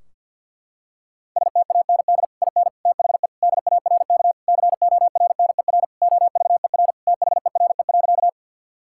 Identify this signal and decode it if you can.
Morse code, heard as a keyed tone, about 50 wpm, 700 Hz. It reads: STAND IN THE DARK CORNER OVER THERE.
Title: Sentences from Top 1600 Words 50wpm